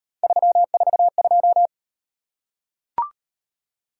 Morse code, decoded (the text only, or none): 342